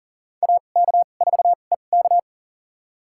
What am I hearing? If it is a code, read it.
Morse code, heard as a keyed tone, about 40 wpm, spaced as for 29 wpm, 700 Hz. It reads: AK4EK
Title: USA Call Signs - 5 Characters - 2x Character Spacing - 40wpm